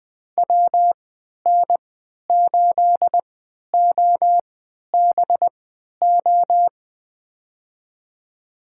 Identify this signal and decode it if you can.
Morse code, heard as a keyed tone, about 20 wpm, spaced as for 11 wpm, 700 Hz. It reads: WN8OBO
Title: USA Call Signs - 6 Characters - 3x Character Spacing - 20wpm